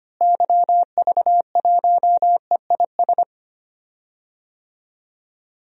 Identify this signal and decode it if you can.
Morse code, heard as a keyed tone, about 25 wpm, 700 Hz. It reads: YV1EIS
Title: DXCC-100 Call Signs - 6 Characters - Repeated 3x - 25wpm